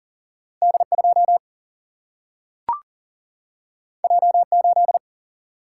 Morse code, heard as a keyed tone, about 40 wpm, 700 Hz. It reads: D2 J8
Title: Single-Letter Single-Number 40wpm